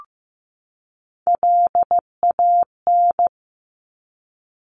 Morse code, heard as a keyed tone, about 15 wpm, 700 Hz. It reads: LAN
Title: Top Three Letter Combinations in Words 15wpm